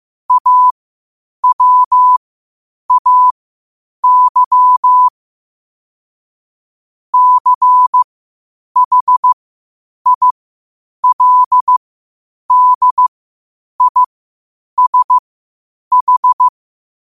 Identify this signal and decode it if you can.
Morse code, heard as a keyed tone, about 15 wpm, spaced as for 8 wpm, 1000 Hz. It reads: AWAY CHILDISH
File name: qotdfiles / 8wpm